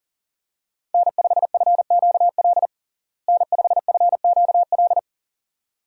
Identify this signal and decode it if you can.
Morse code, heard as a keyed tone, about 40 wpm, 700 Hz. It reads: N5FQL N5FQL
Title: USA Call Signs - 5 Characters - Repeated 2x - 40wpm